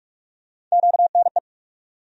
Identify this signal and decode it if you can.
Morse code, heard as a keyed tone, about 47 wpm, 700 Hz. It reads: QNE
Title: USA Call Sign Postfixes - 3 Letters 45wpm